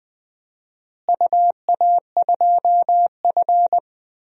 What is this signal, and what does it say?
Morse code, heard as a keyed tone, about 19 wpm, 700 Hz. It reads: UA2F
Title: DXCC-100 Call Signs - 4 Characters - Repeated 3x - 20wpm